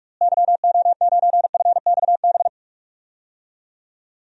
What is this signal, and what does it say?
Morse code, heard as a keyed tone, about 46 wpm, 700 Hz. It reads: YO9FXB
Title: DXCC-100 Call Signs - 6 Characters - Repeated 3x - 45wpm